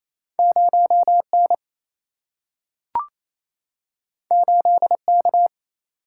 Morse code, heard as a keyed tone, about 28 wpm, 700 Hz. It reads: 0N 8K
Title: Single-Letter Single-Number 28wpm